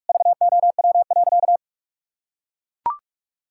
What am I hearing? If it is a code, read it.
Morse code, heard as a keyed tone, about 41 wpm, 700 Hz. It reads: VOW.
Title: Pangram Challenge v2 - 45wpm